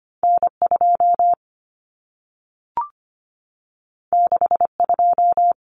Morse code, heard as a keyed tone, about 25 wpm, 700 Hz. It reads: N2 62
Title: Common Cut Numbers - 2-Digit 25wpm